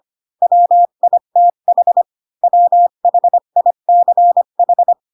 Morse code, heard as a keyed tone, about 25 wpm, 700 Hz. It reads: WITH WHICH